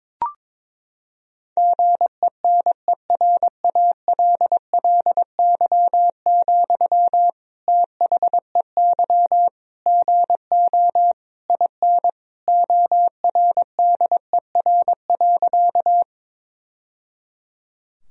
Morse code, heard as a keyed tone, about 22 wpm, 700 Hz. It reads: GENERALLY, THEY GO IN ORDER.
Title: Sentences from Top 1000 Words - Review Part 4 - 22wpm